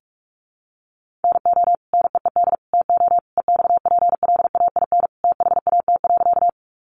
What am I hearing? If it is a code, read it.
Morse code, heard as a keyed tone, about 44 wpm, 700 Hz. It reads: NO NEED TO EXPLAIN THAT.
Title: Sentences from Top 1100 Words 45wpm